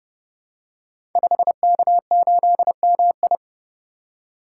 Morse code, heard as a keyed tone, about 30 wpm, 700 Hz. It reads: 5K8MI
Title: Code Groups - 5 Letter:Numbers - 30wpm